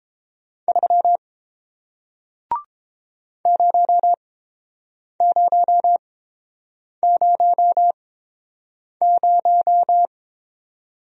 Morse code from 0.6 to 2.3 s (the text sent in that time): 3